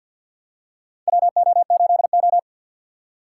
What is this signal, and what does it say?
Morse code, heard as a keyed tone, about 47 wpm, 700 Hz. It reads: WO8O
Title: USA Call Signs - 4 Characters - 2x Character Spacing - 50wpm